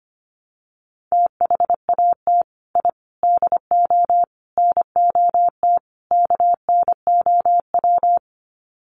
Morse code, heard as a keyed tone, about 25 wpm, 700 Hz. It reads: THAT I DO NOT KNOW
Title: Sets of 5 Words Encore2 25wpm